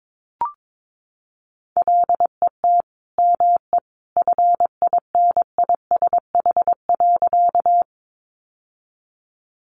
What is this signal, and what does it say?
Morse code, heard as a keyed tone, about 22 wpm, 700 Hz. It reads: LET ME FINISH.
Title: Sentences from Top 1200 Words 22wpm